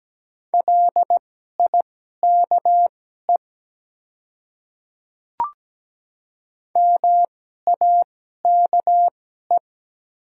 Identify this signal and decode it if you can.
Morse code, heard as a keyed tone, about 17 wpm, spaced as for 12 wpm, 700 Hz. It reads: LIKE MAKE
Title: 2x Character Spacing - Top 100 Words - 17wpm